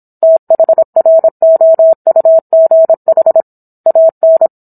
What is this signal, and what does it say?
Morse code, heard as a keyed tone, about 27 wpm, 650 Hz. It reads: THROUGH AN
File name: Book One / 17-Chapter 17 - 26 WPM